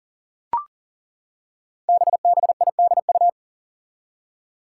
Morse code, heard as a keyed tone, about 38 wpm, 700 Hz. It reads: BBIDU
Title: Five Challenging Letter-Number Combinations 40wpm